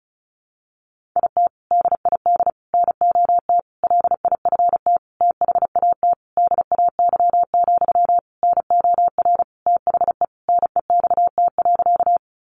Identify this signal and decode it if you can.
Morse code, heard as a keyed tone, about 34 wpm, 700 Hz. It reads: IT DID NOT LIFT THAT DAY, NOR THE NEXT.
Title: Sentences from Top 1500 Words 35wpm